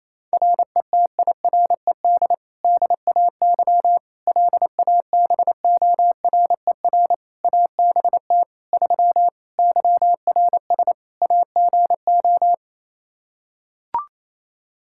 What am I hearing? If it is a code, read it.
Morse code, heard as a keyed tone, about 28 wpm, 700 Hz. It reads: RETIRED DAY LABORER ABT 3 YRS AGO
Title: Practice QSO - Exchange 3 - More Info - 28wpm